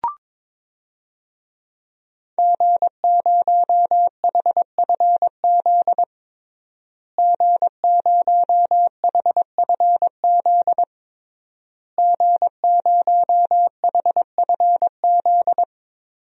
Morse code, heard as a keyed tone, about 22 wpm, 700 Hz. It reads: G0HFZ G0HFZ G0HFZ
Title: DXCC-100 Call Signs - 5 Characters - Repeated 3x - 22wpm